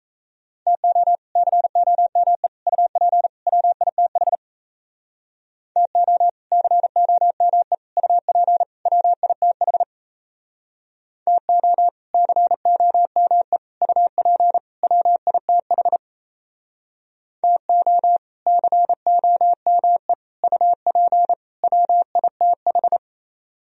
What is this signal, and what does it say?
Morse code, heard as a keyed tone, about 36 wpm, 700 Hz. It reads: TO COME UP WITH TO COME UP WITH TO COME UP WITH TO COME UP WITH